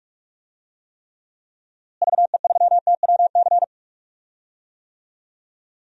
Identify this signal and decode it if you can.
Morse code, heard as a keyed tone, about 46 wpm, 700 Hz. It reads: VE3TWC
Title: DXCC-100 Call Signs - 6 Characters - Repeated 3x - 45wpm